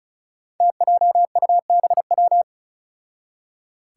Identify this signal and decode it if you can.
Morse code, heard as a keyed tone, about 36 wpm, 700 Hz. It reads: TJUBW
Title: Five Challenging Letter-Number Combinations 35wpm